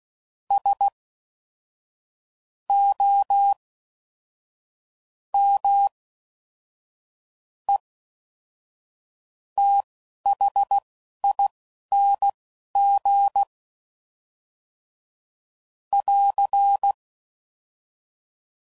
Morse code, heard as a keyed tone, about 16 wpm, spaced as for 6 wpm, 800 Hz. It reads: SOMETHING <AR>